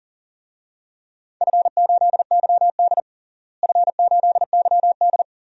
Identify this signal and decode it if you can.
Morse code, heard as a keyed tone, about 40 wpm, 700 Hz. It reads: F8YD F8YD
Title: DXCC-100 Call Signs - 4 Characters - Repeated 2x - 40wpm